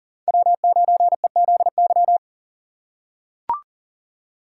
Morse code, heard as a keyed tone, about 40 wpm, 700 Hz. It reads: W9EZY
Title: USA Call Signs - 5 Characters - 3x Character Spacing - 40wpm